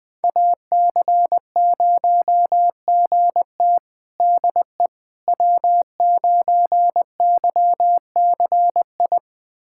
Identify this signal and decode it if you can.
Morse code, heard as a keyed tone, about 21 wpm, 700 Hz. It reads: AC0GT DE W9YCI
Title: USA Callsign DE Callsign - 5 Characters - 20wpm